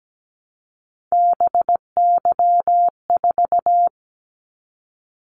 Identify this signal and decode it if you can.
Morse code, heard as a keyed tone, about 17 wpm, 700 Hz. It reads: BY4
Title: Callsign Prefixes for Top 100 DXCC - 17wpm